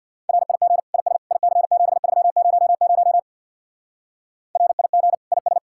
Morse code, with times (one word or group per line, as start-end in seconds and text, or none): 0.3-0.8 RIG
0.9-1.2 IS
1.3-3.2 IC7300
4.6-5.1 RIG
5.3-5.6 IS